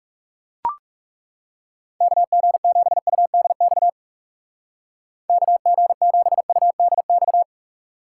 Morse code, heard as a keyed tone, about 40 wpm, 700 Hz. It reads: KG7UDX KG7UDX